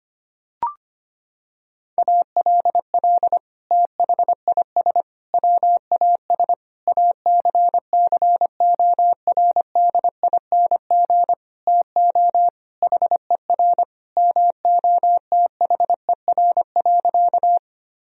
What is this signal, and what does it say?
Morse code, heard as a keyed tone, about 25 wpm, 700 Hz. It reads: ALL THIS WAS ACCORDING TO HER MOTHER.
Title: Sentences from Top 1000 Words - Review Part 4 - 25wpm